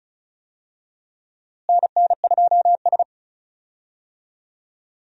Morse code, heard as a keyed tone, about 36 wpm, 700 Hz. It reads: NN2S
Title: USA Call Signs - 4 Characters - Repeated 3x - 35wpm